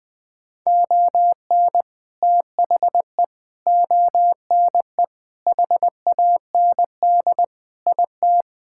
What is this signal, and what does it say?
Morse code, heard as a keyed tone, about 20 wpm, 700 Hz. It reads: ON THE ONE HAND IT
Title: Sentences with 4-Word Phrases - Spoken Beginning - 20wpm